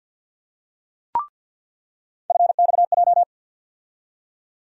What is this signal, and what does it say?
Morse code, heard as a keyed tone, about 48 wpm, 700 Hz. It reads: FXJ
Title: USA Call Sign Postfixes - 3 Letters 50wpm